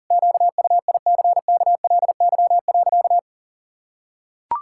Morse code, heard as a keyed tone, about 40 wpm, 700 Hz. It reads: QUICKLY.
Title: Pangram Challenge v2 - 40wpm